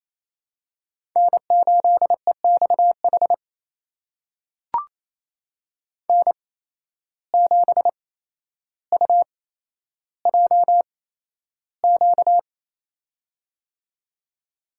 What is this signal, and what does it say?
Morse code, heard as a keyed tone, about 28 wpm, spaced as for 12 wpm, 700 Hz. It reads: N8EXH N7UJQ
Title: USA Call Signs - 5 Characters - 8x Character Spacing - 28wpm